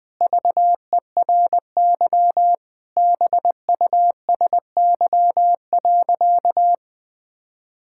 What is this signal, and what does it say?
Morse code, heard as a keyed tone, about 20 wpm, 700 Hz. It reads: VERY BUSY.
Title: Sentences with 7-Word Phrases - 20wpm